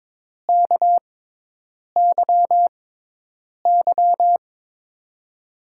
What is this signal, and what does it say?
Morse code, heard as a keyed tone, about 22 wpm, 700 Hz. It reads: K Y Y
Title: USA Call Signs - 6x Character Spacing - 22wpm